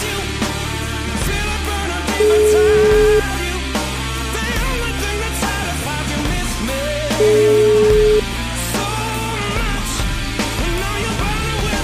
Ringback tone (Europe, 425 Hz), with noise.